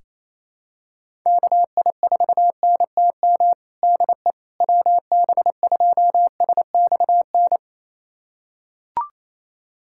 Morse code, heard as a keyed tone, about 28 wpm, 700 Hz. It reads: KI4NTM DE WB2SXN
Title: USA Callsign DE Callsign 28wpm